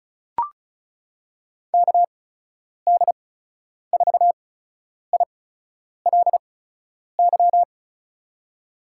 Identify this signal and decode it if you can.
Morse code, heard as a keyed tone, about 34 wpm, spaced as for 10 wpm, 700 Hz. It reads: KD4ILY
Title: USA Call Signs - 6 Characters - 8x Character Spacing - 35wpm